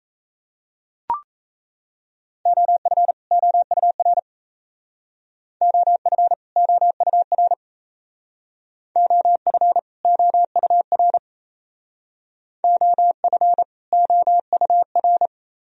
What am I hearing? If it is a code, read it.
Morse code, heard as a keyed tone, about 35 wpm, 700 Hz. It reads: OF OUR OF OUR OF OUR OF OUR